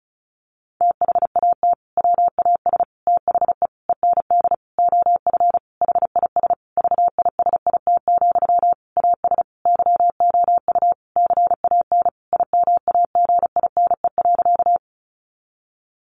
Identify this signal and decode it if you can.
Morse code, heard as a keyed tone, about 34 wpm, 700 Hz. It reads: THAT WAS THE END OF HIS VISIT, AS YOU CAN IMAGINE.
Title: Sentences from Top 1400 Words 35wpm